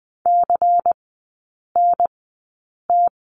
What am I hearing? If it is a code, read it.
Morse code, heard as a keyed tone, about 20 wpm, spaced as for 8 wpm, 700 Hz. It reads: CNT
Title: 093 Lesson 16e - Letter c - Three Characters 20wpm